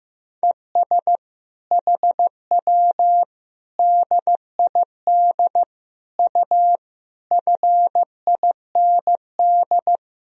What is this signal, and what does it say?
Morse code, heard as a keyed tone, about 15 wpm, 700 Hz. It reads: ES HW DID U FIND
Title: Sentences from Top 100 Words and Abbreviations 15wpm